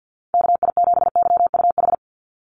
Morse code, heard as a keyed tone, about 50 wpm, 700 Hz. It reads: KI7YUH
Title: USA Call Signs 50wpm